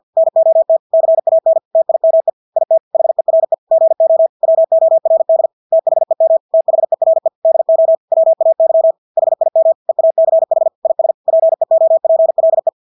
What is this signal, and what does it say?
Morse code, heard as a keyed tone, about 50 wpm, 650 Hz. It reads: NOT CAN TIME IT HERE GO WORD THEM THERE DO WAY HIM EACH IS PEOPLE
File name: Engelske ord 50 wpm